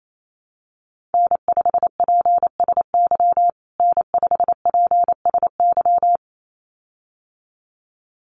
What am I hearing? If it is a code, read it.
Morse code, heard as a keyed tone, about 28 wpm, 700 Hz. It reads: N5PSY N5PSY
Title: USA Call Signs Repeated 2x 28wpm